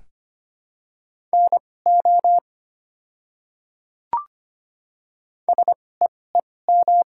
Morse code, heard as a keyed tone, about 25 wpm, spaced as for 18 wpm, 690 Hz. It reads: NO SEEM
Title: 2x Character Spacing - Top 300 Words - 25wpm